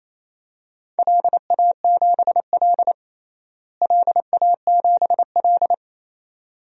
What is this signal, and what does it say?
Morse code, heard as a keyed tone, about 27 wpm, 700 Hz. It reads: LA7L LA7L
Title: DXCC-100 Call Signs - 4 Characters - Repeated 2x - 28wpm